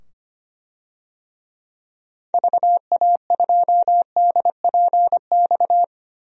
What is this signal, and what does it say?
Morse code, heard as a keyed tone, about 25 wpm, 700 Hz. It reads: VA2DPX
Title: DXCC-100 Call Signs - 6 Characters - 25wpm